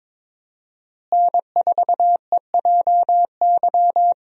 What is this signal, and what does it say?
Morse code, heard as a keyed tone, about 21 wpm, 700 Hz. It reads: N4EJY
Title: USA Call Signs - 5 Characters - Repeated 3x - 22wpm